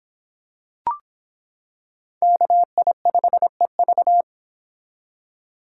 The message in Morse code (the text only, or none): KI5EV